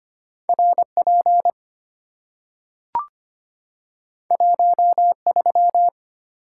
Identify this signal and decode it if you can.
Morse code, heard as a keyed tone, about 25 wpm, 700 Hz. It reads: R P 1 3
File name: Two Letter-Number Combinations - 25wpm